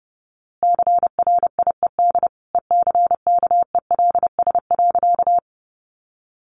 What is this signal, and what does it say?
Morse code, heard as a keyed tone, about 30 wpm, 700 Hz. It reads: CRIED ECKELS.
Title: A Sound Of Thunder 30wpm